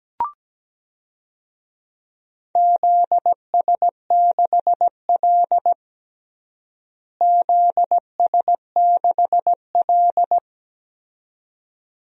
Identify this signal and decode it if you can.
Morse code, heard as a keyed tone, about 17 wpm, 700 Hz. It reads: ZS6L ZS6L
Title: DXCC-100 Call Signs - 4 Characters - Repeated 2x - 17wpm